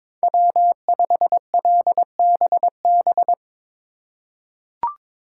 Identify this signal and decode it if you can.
Morse code, heard as a keyed tone, about 23 wpm, 700 Hz. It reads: W5LBB
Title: USA Call Signs - 5 Characters - 3x Character Spacing - 22wpm